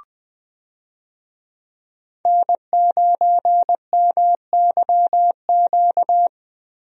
Morse code, heard as a keyed tone, about 20 wpm, 700 Hz. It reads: N9MYQ
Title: USA Call Signs - 5 Characters - 20wpm